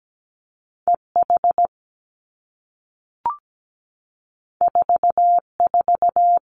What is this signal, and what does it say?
Morse code, heard as a keyed tone, about 17 wpm, 700 Hz. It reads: EH 44
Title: Two Challenging Letter-Number Combinations 17wpm